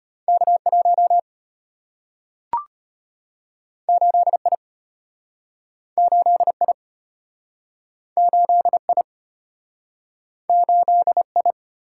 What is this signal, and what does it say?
Morse code, heard as a keyed tone, about 34 wpm, 700 Hz. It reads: K1 8I 8I 8I 8I